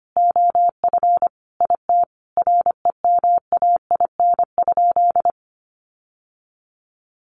Morse code, heard as a keyed tone, about 25 wpm, 700 Hz. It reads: OF IT REMAIN?
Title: Sentences from Top 1200 Words 25wpm